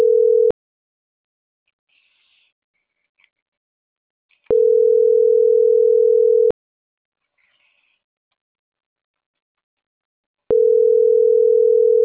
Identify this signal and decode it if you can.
Ringback tone (North America), 440 + 480 Hz.